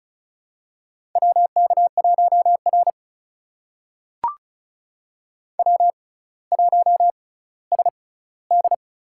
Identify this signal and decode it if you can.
Morse code, heard as a keyed tone, about 34 wpm, spaced as for 18 wpm, 700 Hz. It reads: WK1R W1SD
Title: USA Call Signs - 6x Character Spacing - 35wpm